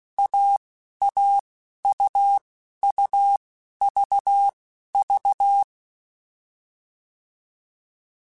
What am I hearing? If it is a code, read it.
Morse code, heard as a keyed tone, about 16 wpm, spaced as for 11 wpm, 800 Hz. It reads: AAUUVV